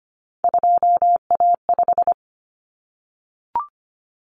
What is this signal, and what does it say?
Morse code, heard as a keyed tone, about 25 wpm, 700 Hz. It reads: 2A5
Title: Common Cut Numbers - 3-Digit 25wpm